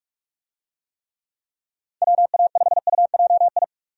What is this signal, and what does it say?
Morse code, heard as a keyed tone, about 46 wpm, 700 Hz. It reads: WA5UJI
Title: USA Call Signs - 6 Characters - 45wpm